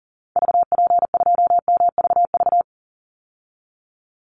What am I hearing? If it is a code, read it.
Morse code, heard as a keyed tone, about 40 wpm, 700 Hz. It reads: VP2MVV
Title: Call Signs Repeated 2x - 40wpm